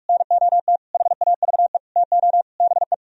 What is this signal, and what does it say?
Morse code, heard as a keyed tone, about 46 wpm, 700 Hz. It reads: NOT HAVE TO BE